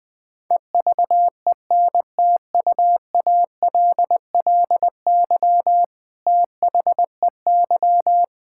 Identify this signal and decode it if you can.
Morse code, heard as a keyed tone, about 21 wpm, 700 Hz. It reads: EVENTUALLY THEY
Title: Sentences from Top 1000 Words - Review Part 6 - 20wpm